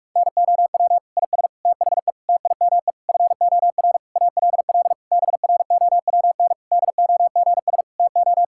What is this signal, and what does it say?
Morse code, heard as a keyed tone, about 46 wpm, 700 Hz. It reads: NOW IS THE TIME FOR ALL BROWN DOGS TO
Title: Pangram Challenge v2 - 45wpm